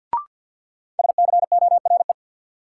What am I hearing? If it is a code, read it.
Morse code, heard as a keyed tone, about 48 wpm, 700 Hz. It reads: SCORE
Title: Sentences from Top 1600 Words - New Word Drill - 50wpm